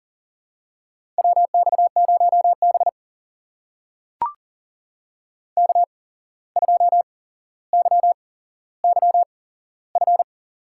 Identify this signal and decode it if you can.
Morse code, heard as a keyed tone, about 39 wpm, spaced as for 16 wpm, 700 Hz. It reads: WX0B K2YYF
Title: USA Call Signs - 8x Character Spacing - 40wpm